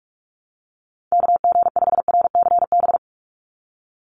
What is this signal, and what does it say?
Morse code, heard as a keyed tone, about 46 wpm, 700 Hz. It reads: KG5RCB
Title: USA Call Signs 45wpm